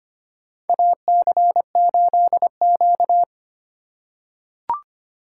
Morse code, heard as a keyed tone, about 25 wpm, 700 Hz. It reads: AC8Q
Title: USA Call Signs - 4x Character Spacing - 25wpm